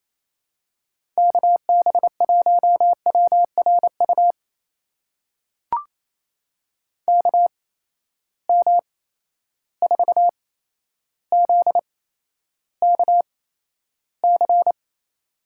Morse code, heard as a keyed tone, about 28 wpm, spaced as for 12 wpm, 700 Hz. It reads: KB1WRU KM4ZKC
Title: USA Call Signs - 6 Characters - 8x Character Spacing - 28wpm